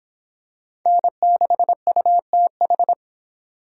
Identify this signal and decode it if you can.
Morse code, heard as a keyed tone, about 25 wpm, 700 Hz. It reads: N6UTH